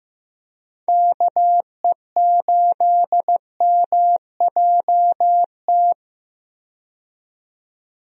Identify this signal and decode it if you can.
Morse code, heard as a keyed tone, about 15 wpm, 700 Hz. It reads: KE8MJT
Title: USA Call Signs - 6 Characters - 15wpm